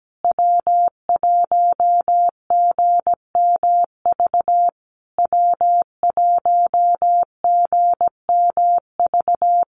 Morse code, heard as a keyed tone, about 17 wpm, 700 Hz. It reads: W1GMV W1GMV
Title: USA Call Signs Repeated 3x 17wpm